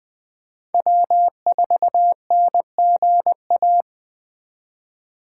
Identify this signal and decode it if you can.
Morse code, heard as a keyed tone, about 20 wpm, 700 Hz. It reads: W4NGA
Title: Speed Racing - USA 1x3 Call Signs 17wpm